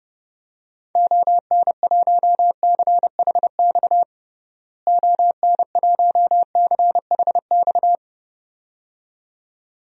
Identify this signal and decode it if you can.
Morse code, heard as a keyed tone, about 30 wpm, 700 Hz. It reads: ON1CHX ON1CHX
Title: DXCC-100 Call Signs - 6 Characters - Repeated 2x - 30wpm